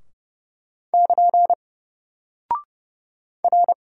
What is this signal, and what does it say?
Morse code, heard as a keyed tone, about 30 wpm, 700 Hz. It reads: <KN> R
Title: Single Character (Letter:Number:Punctuation:Prosign) Alt 30wpm